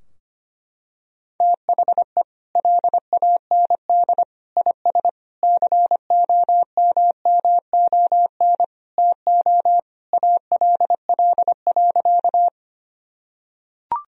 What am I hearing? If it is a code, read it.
Morse code, heard as a keyed tone, about 25 wpm, 700 Hz. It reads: THE LAND IS COMMON TO ALL.
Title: Sentences from Top 1000 Words - Review Part 3 - 25wpm